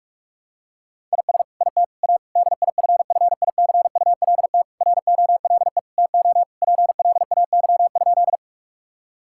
Morse code, heard as a keyed tone, about 46 wpm, 700 Hz. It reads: IS IT A DIFFICULT ROLE TO PLAY?
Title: Sentences from Top 1000 Words - Review Part 6 - 45wpm